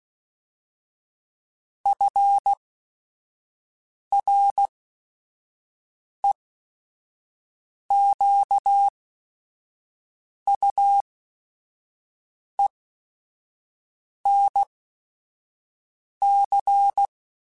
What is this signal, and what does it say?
Morse code, heard as a keyed tone, about 16 wpm, spaced as for 5 wpm, 800 Hz. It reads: FREQUENC
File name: Technical Subjects / Technical Subjects-mp3s / XC @5